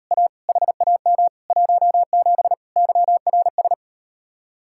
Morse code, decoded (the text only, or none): A HAM 17 YRS